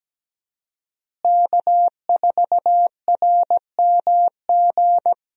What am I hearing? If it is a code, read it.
Morse code, heard as a keyed tone, about 17 wpm, 700 Hz. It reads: K4RMG
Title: USA Call Signs - 5 Characters - Repeated 3x - 17wpm